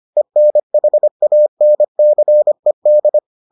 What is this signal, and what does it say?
Morse code, heard as a keyed tone, about 25 wpm, 600 Hz. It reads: ENHANCED